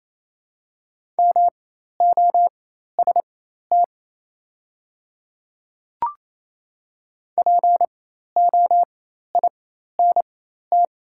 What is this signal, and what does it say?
Morse code, heard as a keyed tone, about 28 wpm, spaced as for 13 wpm, 700 Hz. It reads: MOST POINT
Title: 4x Character Spacing - Top 200 Words - 28wpm